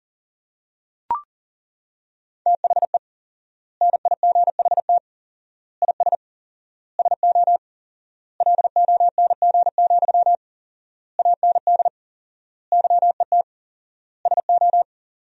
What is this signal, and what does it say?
Morse code, heard as a keyed tone, about 37 wpm, 700 Hz. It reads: THE NIGHT IS SO LONG, AND YET SO
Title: 4x Word-Spacing - Sentences from Top 500 Words - 40wpm